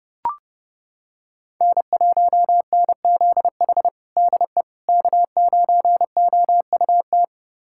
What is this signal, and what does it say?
Morse code, heard as a keyed tone, about 30 wpm, 700 Hz. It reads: N1NZH DE K9OUT
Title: USA Callsign DE Callsign - 5 Characters - 30wpm